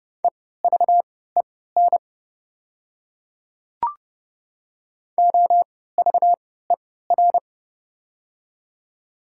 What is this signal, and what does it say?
Morse code, heard as a keyed tone, about 30 wpm, spaced as for 17 wpm, 700 Hz. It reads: EVEN OVER
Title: 3x Character Spacing - Top 500 Words - 30wpm